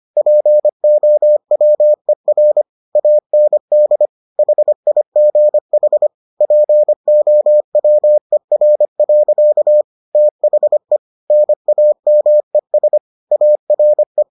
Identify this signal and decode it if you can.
Morse code, heard as a keyed tone, about 25 wpm, 600 Hz. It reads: POWER AND HIGH POWER. THE NAMES ARE